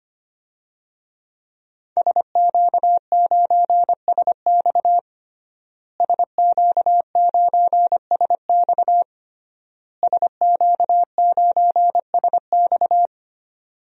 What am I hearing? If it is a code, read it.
Morse code, heard as a keyed tone, about 25 wpm, 700 Hz. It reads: SQ9SX SQ9SX SQ9SX
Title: DXCC-100 Call Signs - 5 Characters - Repeated 3x - 25wpm